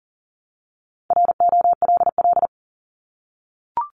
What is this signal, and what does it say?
Morse code, heard as a keyed tone, about 41 wpm, 700 Hz. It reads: ROLL
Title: Multi-letter Phonemes 40wpm